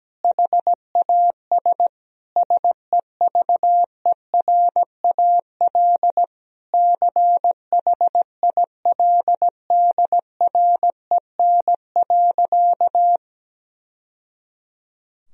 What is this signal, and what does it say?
Morse code, heard as a keyed tone, about 17 wpm, 700 Hz. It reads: HAS SEVERAL CHILDREN.
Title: Sentences from Top 1000 Words - Review Part 2 - 17wpm